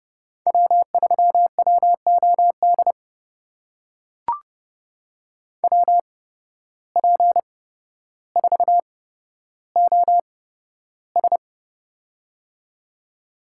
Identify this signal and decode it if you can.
Morse code, heard as a keyed tone, about 32 wpm, spaced as for 13 wpm, 700 Hz. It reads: W3WOD WP4OS
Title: USA Call Signs - 5 Characters - 8x Character Spacing - 30wpm